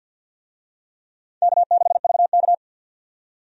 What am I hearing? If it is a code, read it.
Morse code, heard as a keyed tone, about 46 wpm, 700 Hz. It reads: K6VK
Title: USA Call Signs - 4 Characters - 50wpm